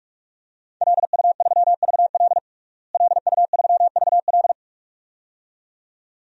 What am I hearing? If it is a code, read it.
Morse code, heard as a keyed tone, about 44 wpm, 700 Hz. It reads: LU3VL LU3VL
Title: DXCC-100 Call Signs - 5 Characters - Repeated 2x - 45wpm